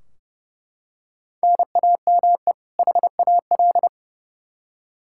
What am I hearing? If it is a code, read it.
Morse code, heard as a keyed tone, about 30 wpm, 700 Hz. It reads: NAME HAL
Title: Practice QSO - Exchange 1 - Beginning - 30wpm